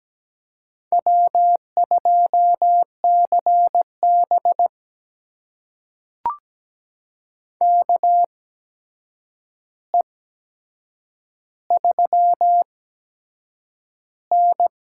Morse code, heard as a keyed tone, about 17 wpm, spaced as for 7 wpm, 700 Hz. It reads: W2CB KE3N
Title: USA Call Signs - 4 Characters - 8x Character Spacing - 17wpm